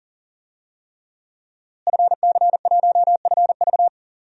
Morse code, heard as a keyed tone, about 40 wpm, 700 Hz. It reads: FC1FV
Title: DXCC-100 Call Signs - 5 Characters - Repeated 3x - 40wpm